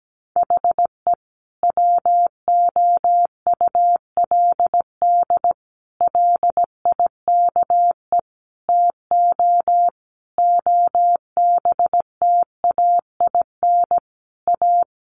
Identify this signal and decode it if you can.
Morse code, heard as a keyed tone, about 17 wpm, 700 Hz. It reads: HE WOULD LIKE TO OBTAIN A
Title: Sentences from Top 1300 Words 17wpm